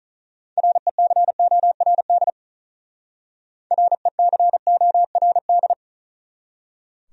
Morse code, heard as a keyed tone, about 45 wpm, 700 Hz. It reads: RECORD RECORD